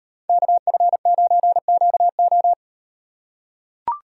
Morse code, heard as a keyed tone, about 37 wpm, 700 Hz. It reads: KF9QO